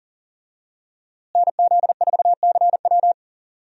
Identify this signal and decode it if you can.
Morse code, heard as a keyed tone, about 40 wpm, 700 Hz. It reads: NZ4CW
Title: USA Call Signs - 5 Characters - Repeated 2x - 40wpm